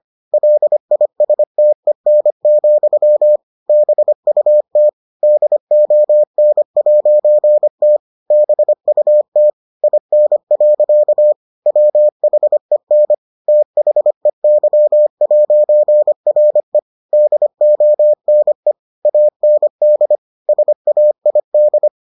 Morse code, heard as a keyed tone, about 25 wpm, 600 Hz. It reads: LISTEN, BUT DON'T BUT IN. WHEN THEY'RE DONE AND SAID